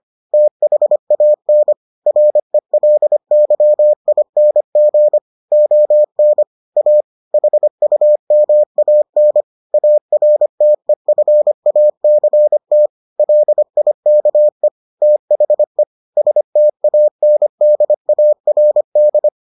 Morse code, heard as a keyed tone, about 25 wpm, 600 Hz. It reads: THAN RELYING ON A HUMAN ARTEFACT LIKE THE STANDARD